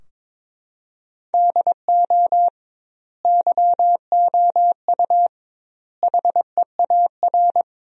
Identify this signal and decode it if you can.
Morse code, heard as a keyed tone, about 21 wpm, 700 Hz. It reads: DO YOU HEAR
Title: 2x Word-Spacing - Sentences from Top 300 Words - 22wpm